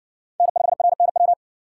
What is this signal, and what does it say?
Morse code, heard as a keyed tone, about 64 wpm, 700 Hz. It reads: N5RNW